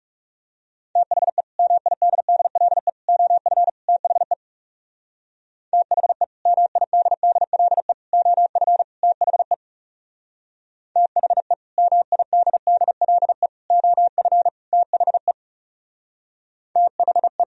Morse code, heard as a keyed tone, about 37 wpm, 700 Hz. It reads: THE MIDDLE OF THE THE MIDDLE OF THE THE MIDDLE OF THE THE
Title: Speed-Racing - Sets of 4 Words 30wpm